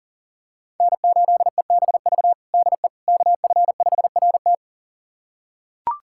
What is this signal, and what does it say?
Morse code, heard as a keyed tone, about 40 wpm, 700 Hz. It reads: N8EBV DE KF5RT
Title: USA Callsign DE Callsign - 5 Characters - 40wpm